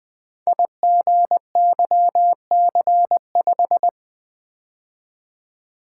Morse code, heard as a keyed tone, about 20 wpm, 700 Hz. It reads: IGYC5
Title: Code Groups - 5 Letter:Numbers - 20wpm